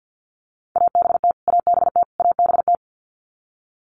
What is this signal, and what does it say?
Morse code, heard as a keyed tone, about 50 wpm, 700 Hz. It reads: ABT ABT ABT
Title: Abbreviations Repeated 3x - 50wpm